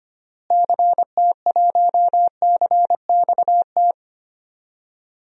Morse code, heard as a keyed tone, about 25 wpm, 700 Hz. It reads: CT1CXT
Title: DXCC-100 Call Signs - 6 Characters - Repeated 3x - 25wpm